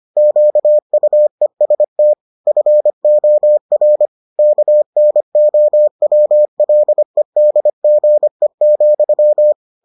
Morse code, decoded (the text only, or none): QUEST FOR KNOWLEDGE,